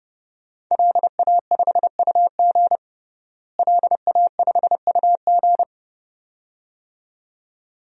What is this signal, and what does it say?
Morse code, heard as a keyed tone, about 30 wpm, 700 Hz. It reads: LA5UG LA5UG
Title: DXCC-100 Call Signs - 5 Characters - Repeated 2x - 30wpm